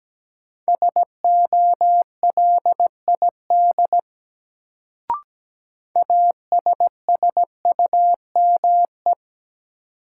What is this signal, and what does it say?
Morse code, heard as a keyed tone, about 17 wpm, 700 Hz. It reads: SOLID ASSUME
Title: Sentences from Top 1500 Words - New Word Drill - 17wpm